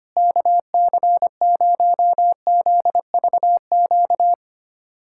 Morse code, heard as a keyed tone, about 25 wpm, 700 Hz. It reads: KC0ZVQ
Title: USA Callsign DE Callsign 25wpm